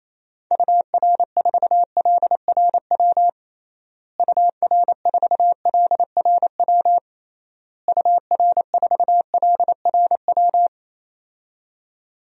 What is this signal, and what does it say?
Morse code, heard as a keyed tone, about 28 wpm, 700 Hz. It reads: UR4LRW UR4LRW UR4LRW
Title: DXCC-100 Call Signs - 6 Characters - Repeated 3x - 28wpm